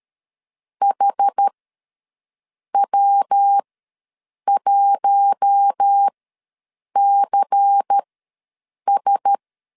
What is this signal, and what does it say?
Morse code, heard as a keyed tone, about 13 wpm, spaced as for 7 wpm, 800 Hz. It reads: HW1CS
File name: LEZIONI / 5 LBVC / 5 LBVC ws 12wpm 3159